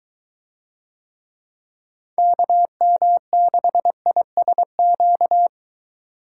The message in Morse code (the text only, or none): KM6ISQ